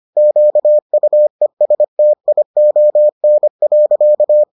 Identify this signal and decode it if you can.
Morse code, heard as a keyed tone, about 25 wpm, 600 Hz. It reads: QUESTION.